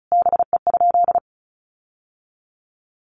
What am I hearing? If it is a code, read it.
Morse code, heard as a keyed tone, about 34 wpm, 700 Hz. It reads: BE?
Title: Sentences from Top 1500 Words 35wpm